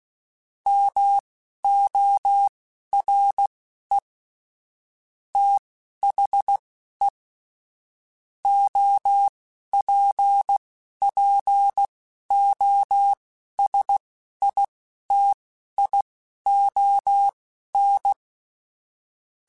Morse code, heard as a keyed tone, about 16 wpm, spaced as for 11 wpm, 800 Hz. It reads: MORE THE OPPOSITION
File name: Technical Subjects / Technical Subjects-mp3s / XL @10